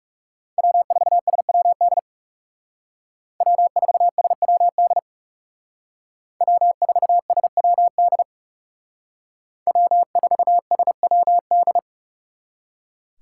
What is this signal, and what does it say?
Morse code, heard as a keyed tone, about 35 wpm, 700 Hz. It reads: W4SWD W4SWD W4SWD W4SWD
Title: Speed Racing - USA Call Signs 30wpm